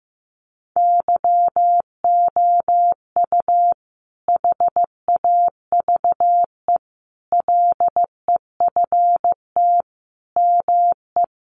Morse code, heard as a keyed tone, about 15 wpm, 700 Hz. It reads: YOU HAVE LEFT ME
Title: Sentences from Top 600 Words 15wpm